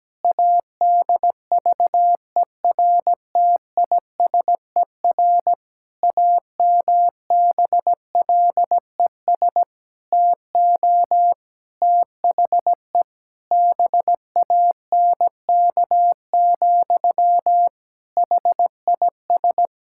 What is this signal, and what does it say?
Morse code, heard as a keyed tone, about 17 wpm, 700 Hz. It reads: ADVERTISER AMBLES TO THE BANK, HIS
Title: Pangram Challenge v2 - 17wpm